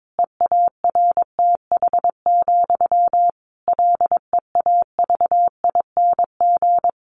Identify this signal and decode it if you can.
Morse code, heard as a keyed tone, about 22 wpm, 700 Hz. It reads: EARTH, LEAVING
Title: A Sound Of Thunder 22wpm